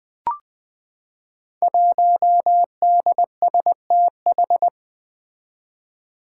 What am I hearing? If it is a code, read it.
Morse code, heard as a keyed tone, about 20 wpm, 700 Hz. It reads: 1DSTH
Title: Five Challenging Letter-Number Combinations 20wpm